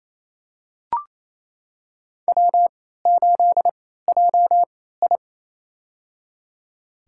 Morse code, heard as a keyed tone, about 28 wpm, spaced as for 16 wpm, 700 Hz. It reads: W8JI
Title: USA Call Signs - 4 Characters - 3x Character Spacing - 28wpm